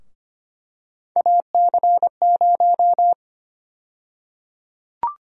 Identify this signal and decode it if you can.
Morse code, heard as a keyed tone, about 25 wpm, 700 Hz. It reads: AC0
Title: USA Call Sign Prefixes - 25wpm